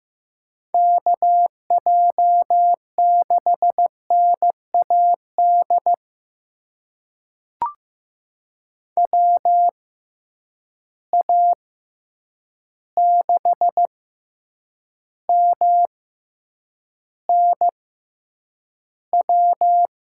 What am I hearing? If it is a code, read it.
Morse code, heard as a keyed tone, about 15 wpm, spaced as for 8 wpm, 700 Hz. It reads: KJ6NAD WA6MNW
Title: USA Call Signs - 6 Characters - 6x Character Spacing - 15wpm